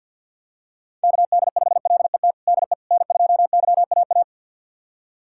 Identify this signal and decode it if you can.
Morse code, heard as a keyed tone, about 47 wpm, 700 Hz. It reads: KD5LET DE N2YAA
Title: USA Callsign DE Callsign 50wpm